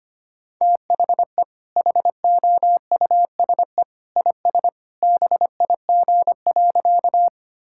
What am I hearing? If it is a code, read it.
Morse code, heard as a keyed tone, about 25 wpm, 700 Hz. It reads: THE HOUSE IS BIG.
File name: practice / 25wpm / Sentences with 2-Word Phrases - 25wpm